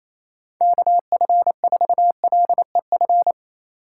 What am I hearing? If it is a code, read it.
Morse code, heard as a keyed tone, about 36 wpm, 700 Hz. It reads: KF4LEF